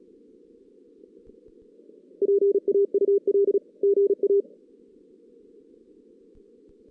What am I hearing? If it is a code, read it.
Morse code, heard as a keyed tone, about 32 wpm, 400 Hz. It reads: PAUL GA
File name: 2016 JAN CW NAQP